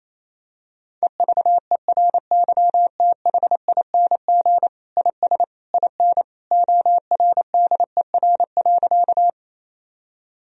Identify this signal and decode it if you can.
Morse code, heard as a keyed tone, about 28 wpm, 700 Hz. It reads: EVERYTHING IS IN ORDER.
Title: Sentences from Top 1000 Words - Review Part 2 - 28wpm